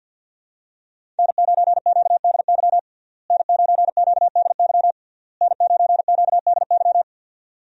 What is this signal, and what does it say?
Morse code, heard as a keyed tone, about 48 wpm, 700 Hz. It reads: N9QDY N9QDY N9QDY
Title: USA Call Signs - 5 Characters - Repeated 3x - 50wpm